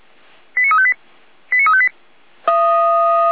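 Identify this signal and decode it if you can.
Five-tone selective call (ZVEI), heard as quick runs of five tones, tones of about 70 ms. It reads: 89358 89358
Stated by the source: ZVEI five-tone call 89230 received by SDR, Feuerwehr Orlamünde.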